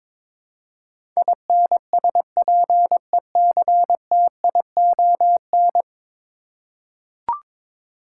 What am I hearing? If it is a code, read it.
Morse code, heard as a keyed tone, about 21 wpm, 700 Hz. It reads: INSPECTION
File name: practice / 22wpm / 10 Letter Words - First 3 Letters Given - 22wpm